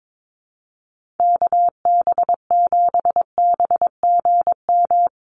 Morse code, heard as a keyed tone, about 22 wpm, 700 Hz. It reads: KB7BGM
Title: USA Call Signs Repeated 3x 22wpm